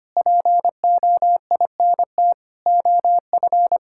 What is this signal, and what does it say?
Morse code, heard as a keyed tone, about 25 wpm, 700 Hz. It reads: POINT OF
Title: Sentences from Top 1000 Words - Review Part 6 - 25wpm